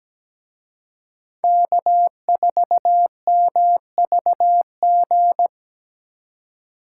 Morse code, heard as a keyed tone, about 20 wpm, 700 Hz. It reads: K4MVG
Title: Speed Racing - USA Call Signs 17wpm